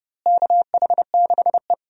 Morse code, heard as a keyed tone, about 30 wpm, 700 Hz. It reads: KH6E
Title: DXCC-100 Call Signs - 4 Characters - Repeated 2x - 30wpm